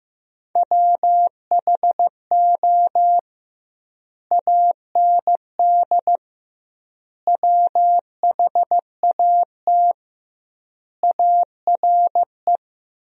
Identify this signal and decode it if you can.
Morse code, heard as a keyed tone, about 16 wpm, 700 Hz. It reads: WHO AND WHAT ARE
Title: 2x Word-Spacing - Sentences from Top 100 Words - 15wpm